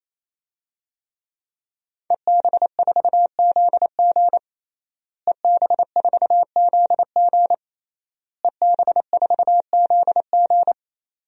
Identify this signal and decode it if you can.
Morse code, heard as a keyed tone, about 28 wpm, 700 Hz. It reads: EB4ZG EB4ZG EB4ZG
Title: DXCC-100 Call Signs - 5 Characters - Repeated 3x - 28wpm